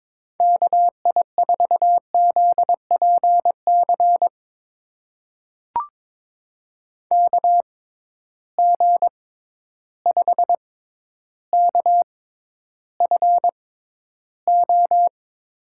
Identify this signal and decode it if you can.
Morse code, heard as a keyed tone, about 21 wpm, spaced as for 11 wpm, 700 Hz. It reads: KI4ZPC KG5KFO
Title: USA Call Signs - 6 Characters - 6x Character Spacing - 22wpm